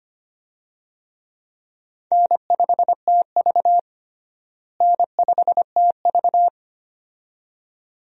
Morse code, heard as a keyed tone, about 25 wpm, 700 Hz. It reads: N5TV N5TV
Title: USA Call Signs - 4 Characters - Repeated 2x - 25wpm